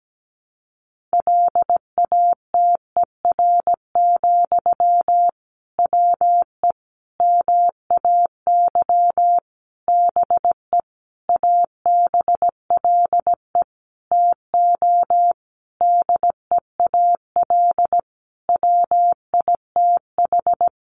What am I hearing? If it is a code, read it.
Morse code, heard as a keyed tone, about 17 wpm, 700 Hz. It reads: LATER, WE MAY BE ABLE TO DEAL WITH
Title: Sentences from Top 800 Words 17wpm